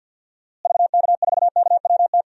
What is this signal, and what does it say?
Morse code, heard as a keyed tone, about 45 wpm, 700 Hz. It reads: VK4KWT
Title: DXCC-100 Call Signs - 6 Characters - Repeated 2x - 50wpm